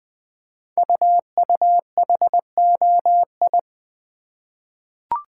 Morse code, heard as a keyed tone, about 20 wpm, 700 Hz. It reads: UUHOI
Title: Code Groups - 5 Letters - 20wpm